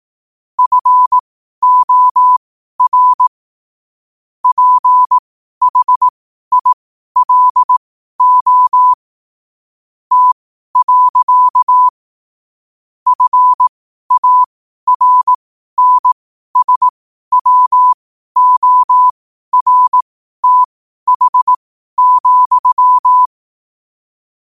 Morse code, heard as a keyed tone, about 18 wpm, spaced as for 12 wpm, 1000 Hz. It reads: FOR PHILO T. FARNSWORTH,
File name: qotdfiles / 12wpm